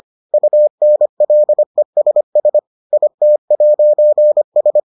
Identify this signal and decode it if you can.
Morse code, heard as a keyed tone, about 25 wpm, 600 Hz. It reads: UNLESS IT'S